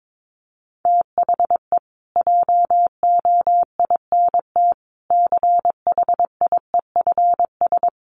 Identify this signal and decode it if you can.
Morse code, heard as a keyed tone, about 22 wpm, 700 Hz. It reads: THE JOINT CHIEFS
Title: Sets of 5 Words Encore2 22wpm